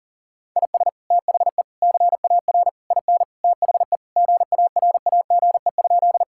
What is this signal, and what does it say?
Morse code, heard as a keyed tone, about 40 wpm, 700 Hz. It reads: IS THE CAR IN THE GARAGE?
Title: Sentences from Top 1600 Words 40wpm